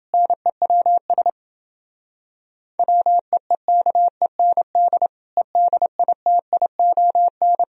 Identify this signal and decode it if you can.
Morse code, heard as a keyed tone, about 29 wpm, 700 Hz. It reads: NEWS WEEKEND EDITION